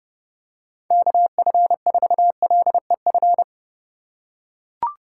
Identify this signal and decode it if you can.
Morse code, heard as a keyed tone, about 28 wpm, 700 Hz. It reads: KF4LEF